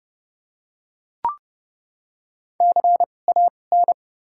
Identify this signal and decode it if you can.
Morse code, heard as a keyed tone, about 30 wpm, spaced as for 22 wpm, 700 Hz. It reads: CAN
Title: 2x Character Spacing - Top 200 Words - 30wpm